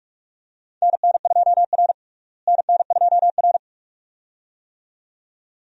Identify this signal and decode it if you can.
Morse code, heard as a keyed tone, about 47 wpm, 700 Hz. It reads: NN2R NN2R
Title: USA Call Signs - 4 Characters - Repeated 2x - 45wpm